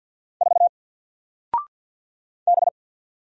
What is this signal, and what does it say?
Morse code, heard as a keyed tone, about 48 wpm, 700 Hz. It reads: 4 B
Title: Single Character (Letter:Number:Punctuation:Prosign) Alt 50wpm